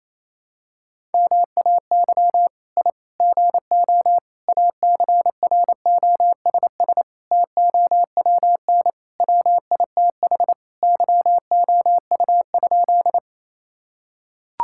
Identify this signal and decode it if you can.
Morse code, heard as a keyed tone, about 27 wpm, 700 Hz. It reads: MAY I GO ACROSS TOWN WITH YOU?
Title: Sentences from Top 1000 Words - Review Part 5 - 28wpm